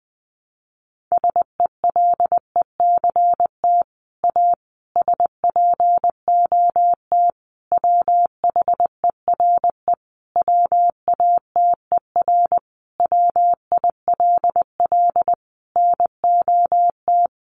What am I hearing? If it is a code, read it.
Morse code, heard as a keyed tone, about 20 wpm, 700 Hz. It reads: SELECT A SPOT WHERE WATER WILL NOT
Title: Sentences from Top 1300 Words 20wpm